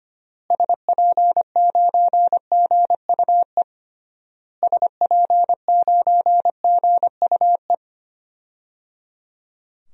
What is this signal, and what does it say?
Morse code, heard as a keyed tone, about 25 wpm, 700 Hz. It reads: SP9GUE SP9GUE
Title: DXCC-100 Call Signs - 6 Characters - Repeated 3x - 25wpm